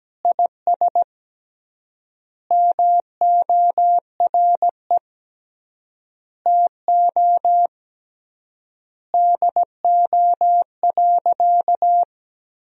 Morse code, heard as a keyed tone, about 17 wpm, 700 Hz. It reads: IS MORE TO DO.
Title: 3x Word-Spacing - Sentences from Top 100 Words - 17wpm